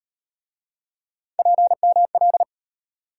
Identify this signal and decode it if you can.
Morse code, heard as a keyed tone, about 36 wpm, 700 Hz. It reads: PML